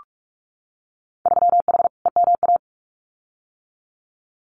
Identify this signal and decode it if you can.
Morse code, heard as a keyed tone, about 46 wpm, 700 Hz. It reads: 3H EMA
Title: Winter Field Day Exchanges Encore 45wpm